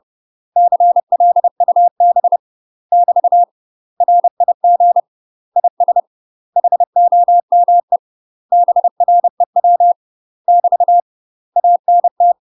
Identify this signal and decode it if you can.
Morse code, heard as a keyed tone, about 30 wpm, 700 Hz. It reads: CLUB <BT> RIG IS HOME BREW <BT> ANT
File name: UK MASTER QSOs TRAINING / 30WPM / UK MASTER QSOs 0031